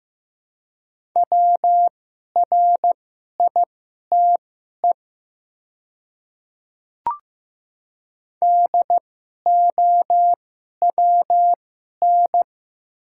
Morse code, heard as a keyed tone, about 15 wpm, spaced as for 11 wpm, 700 Hz. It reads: WRITE DOWN
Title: 2x Character Spacing - Top 100 Words - 15wpm